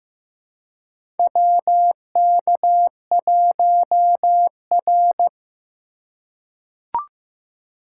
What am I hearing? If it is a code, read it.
Morse code, heard as a keyed tone, about 15 wpm, 700 Hz. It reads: WK1R
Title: USA Call Signs - 3x Character Spacing - 15wpm